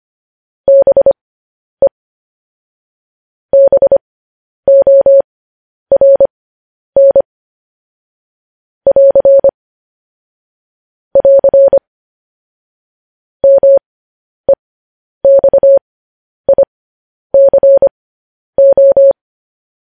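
Morse code, heard as a keyed tone, about 25 wpm, spaced as for 10 wpm, 550 Hz. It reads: BE BORN <AR> <AR> MEXICO
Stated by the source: Morse code news headlines practice transmission.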